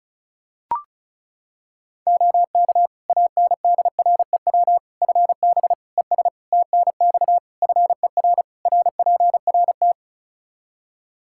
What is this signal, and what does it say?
Morse code, heard as a keyed tone, about 36 wpm, 700 Hz. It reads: OK ANDREW FB ES TNX FER RPRT
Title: Practice QSO - Exchange 2 - Info - 35wpm